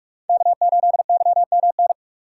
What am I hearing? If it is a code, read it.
Morse code, heard as a keyed tone, about 44 wpm, 700 Hz. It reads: K8YMN